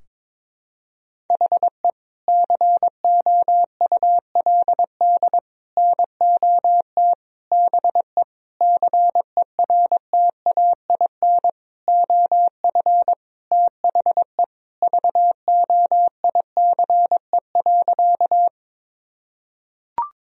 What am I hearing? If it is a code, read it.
Morse code, heard as a keyed tone, about 22 wpm, 700 Hz. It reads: HE COULD NOT BE CERTAIN OF THE VOICE.
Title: Sentences from Top 1000 Words - Review Part 2 - 22wpm